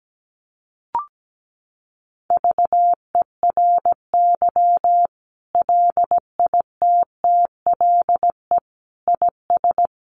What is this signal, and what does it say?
Morse code, heard as a keyed tone, about 17 wpm, 700 Hz. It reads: VERY LITTLE IS
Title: Sentences from Top 1500 Words 17wpm